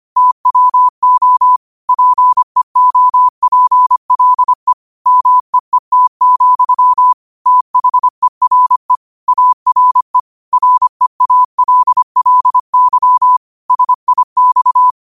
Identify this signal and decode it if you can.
Morse code, heard as a keyed tone, about 25 wpm, 1000 Hz. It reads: TWO PEOPLE MEET, THERE ARE REALLY SIX